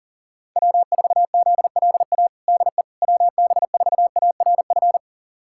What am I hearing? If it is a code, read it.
Morse code, heard as a keyed tone, about 40 wpm, 700 Hz. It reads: W4ZLA DE WB4ARF
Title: USA Callsign DE Callsign 40wpm